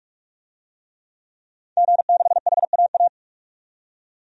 Morse code, heard as a keyed tone, about 46 wpm, 700 Hz. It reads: G6HAA
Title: DXCC-100 Call Signs - 5 Characters - 45wpm